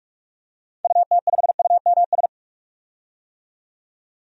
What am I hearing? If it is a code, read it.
Morse code, heard as a keyed tone, about 46 wpm, 700 Hz. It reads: UT5UMS
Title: DXCC-100 Call Signs - 6 Characters - 45wpm